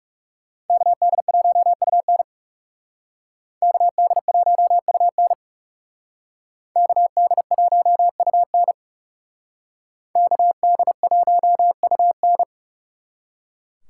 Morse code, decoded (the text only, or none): KD1UN KD1UN KD1UN KD1UN